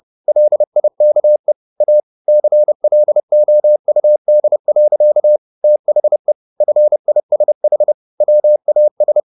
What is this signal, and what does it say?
Morse code, heard as a keyed tone, about 30 wpm, 600 Hz. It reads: LIKE A CLOUD. THE FISH WAS